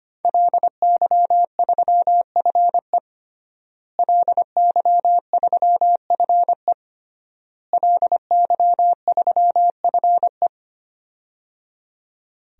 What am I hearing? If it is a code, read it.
Morse code, heard as a keyed tone, about 25 wpm, 700 Hz. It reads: LY3FE LY3FE LY3FE
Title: DXCC-100 Call Signs - 5 Characters - Repeated 3x - 25wpm